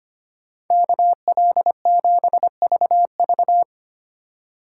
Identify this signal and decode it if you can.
Morse code, heard as a keyed tone, about 25 wpm, 700 Hz. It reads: KL7VV
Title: USA Call Signs - 5 Characters - 2x Character Spacing - 25wpm